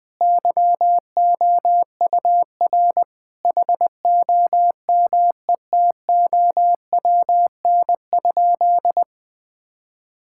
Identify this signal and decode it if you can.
Morse code, heard as a keyed tone, about 20 wpm, 700 Hz. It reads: YOUR HOMETOWN?
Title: Sentences with 8-Word Phrases - 20wpm